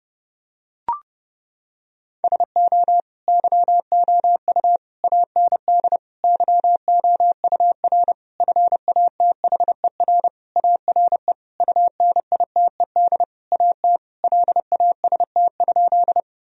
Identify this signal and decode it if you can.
Morse code, heard as a keyed tone, about 30 wpm, 700 Hz. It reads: SO YOU AND YOUR FATHER ARE UNITED AT LAST?
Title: Sentences from Top 1000 Words - Review Part 4 - 30wpm